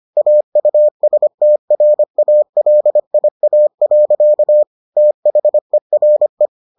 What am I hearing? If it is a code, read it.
Morse code, heard as a keyed tone, about 25 wpm, 600 Hz. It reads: AUSTRALIA. THERE